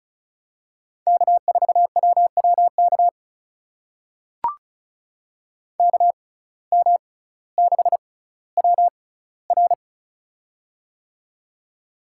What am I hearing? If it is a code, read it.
Morse code, heard as a keyed tone, about 36 wpm, spaced as for 18 wpm, 700 Hz. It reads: K4WWK KM6WR
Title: USA Call Signs - 5 Characters - 6x Character Spacing - 35wpm